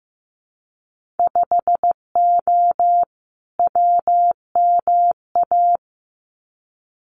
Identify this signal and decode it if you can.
Morse code, heard as a keyed tone, about 15 wpm, 700 Hz. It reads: 5O WMA
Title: Winter Field Day Exchanges Encore 15wpm